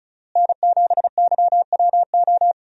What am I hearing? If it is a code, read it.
Morse code, heard as a keyed tone, about 36 wpm, 700 Hz. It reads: N7YWO
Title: USA Call Signs - 5 Characters - Repeated 3x - 35wpm